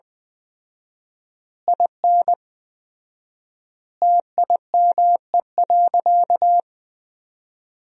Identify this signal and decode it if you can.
Morse code, heard as a keyed tone, about 20 wpm, 700 Hz. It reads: IN TIME.
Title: 4x Word-Spacing - Sentences from Top 300 Words - 20wpm